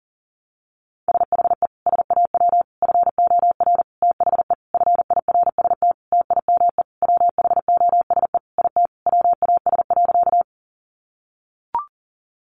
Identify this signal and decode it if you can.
Morse code, heard as a keyed tone, about 40 wpm, 700 Hz. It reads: SHE SAW FOR THE FIRST TIME WHOSE IT WAS.
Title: Sentences from Top 1200 Words 40wpm